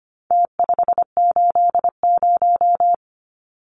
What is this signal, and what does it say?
Morse code, heard as a keyed tone, about 25 wpm, 700 Hz. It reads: T580
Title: Numbers - Contest Style - 4-Digit 25wpm